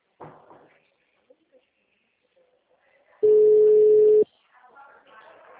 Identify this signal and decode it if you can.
Ringback tone (Europe, 425 Hz), with speech.